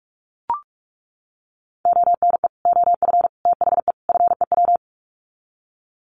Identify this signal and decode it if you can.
Morse code, heard as a keyed tone, about 44 wpm, 700 Hz. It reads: ONE OF THE FEW
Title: Sets of 4 Words 45wpm